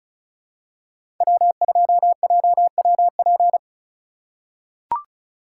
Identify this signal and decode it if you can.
Morse code, heard as a keyed tone, about 36 wpm, 700 Hz. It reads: W2JWP
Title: USA Call Signs - 5 Characters - 2x Character Spacing - 35wpm